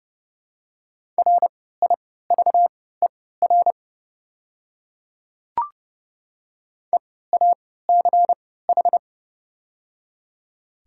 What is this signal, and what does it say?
Morse code, heard as a keyed tone, about 30 wpm, spaced as for 17 wpm, 700 Hz. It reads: RIVER EACH
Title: 3x Character Spacing - Top 500 Words - 30wpm